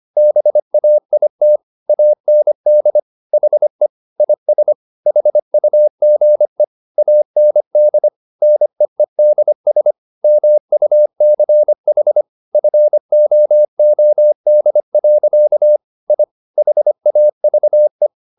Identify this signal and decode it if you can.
Morse code, heard as a keyed tone, about 25 wpm, 600 Hz. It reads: BAIT AND HE IS HUGE AND NEEDS MUCH FOOD. I HAVE